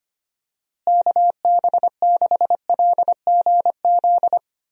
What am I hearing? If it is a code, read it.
Morse code, heard as a keyed tone, about 25 wpm, 700 Hz. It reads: KB6LGZ
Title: USA Call Signs - 6 Characters - Repeated 2x - 25wpm